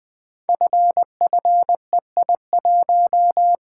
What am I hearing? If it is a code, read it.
Morse code, heard as a keyed tone, about 20 wpm, 700 Hz. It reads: FFEI1
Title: Code Groups - 5 Letter:Numbers - 20wpm